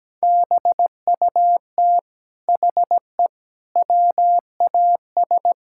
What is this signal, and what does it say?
Morse code, heard as a keyed tone, about 17 wpm, 700 Hz. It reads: BUT HE WAS
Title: Sentences from Top 1000 Words - Review Part 5 - 17wpm